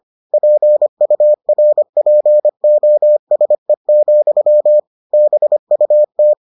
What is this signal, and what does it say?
Morse code, heard as a keyed tone, about 25 wpm, 600 Hz. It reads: PURPOSE, BUT